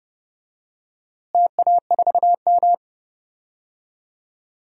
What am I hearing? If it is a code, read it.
Morse code, heard as a keyed tone, about 29 wpm, 700 Hz. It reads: TA4M
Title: DXCC-100 Call Signs - 4 Characters - Repeated 3x - 30wpm